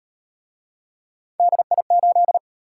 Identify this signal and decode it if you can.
Morse code, heard as a keyed tone, about 36 wpm, 700 Hz. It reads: DI8